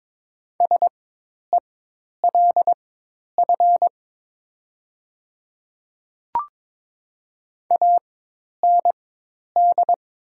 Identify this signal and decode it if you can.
Morse code, heard as a keyed tone, about 22 wpm, spaced as for 10 wpm, 700 Hz. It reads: SELF AND
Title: 4x Character Spacing - Top 200 Words - 22wpm